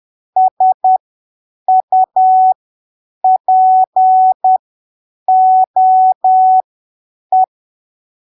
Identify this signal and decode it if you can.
Morse code, heard as a keyed tone, about 10 wpm, spaced as for 7 wpm, 750 Hz. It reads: SUPOE
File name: arquivos / texto / texto 26 10ppm